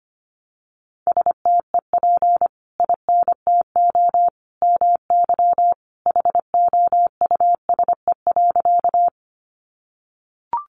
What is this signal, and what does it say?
Morse code, heard as a keyed tone, about 25 wpm, 700 Hz. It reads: STEP INTO MY HOUSE.
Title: Sentences from Top 600 Words 25wpm